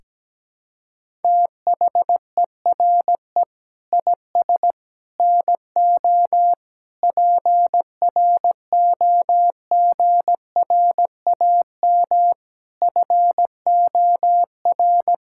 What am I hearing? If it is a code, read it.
Morse code, heard as a keyed tone, about 17 wpm, 700 Hz. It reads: THERE IS NO PROGRAM FOR
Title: Sentences from Top 1000 Words - Review Part 5 - 17wpm